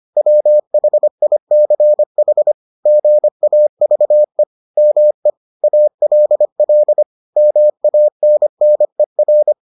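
Morse code, heard as a keyed tone, about 25 wpm, 600 Hz. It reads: WHICH GAVE ME ALL MANNER
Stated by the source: Morse edition of the Foundations of Amateur Radio episodes.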